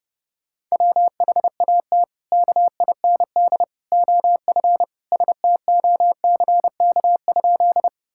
Morse code, heard as a keyed tone, about 32 wpm, 700 Hz. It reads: WHAT KIND OF STOCK?
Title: Sentences from Top 1000 Words - Review Part 5 - 30wpm